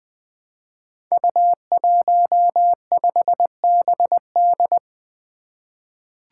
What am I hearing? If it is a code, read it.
Morse code, heard as a keyed tone, about 20 wpm, 700 Hz. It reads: U15BD
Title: Five Challenging Letter-Number Combinations 20wpm